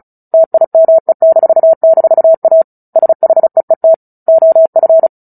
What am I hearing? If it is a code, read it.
Morse code, heard as a keyed tone, about 34 wpm, 650 Hz. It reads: TIME--A SHEET OF
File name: Book Two / 09-Chapter 9 - 35 WPM